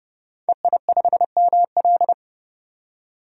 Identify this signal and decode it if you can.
Morse code, heard as a keyed tone, about 30 wpm, 700 Hz. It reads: EI5ML
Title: DXCC-100 Call Signs - 5 Characters - 30wpm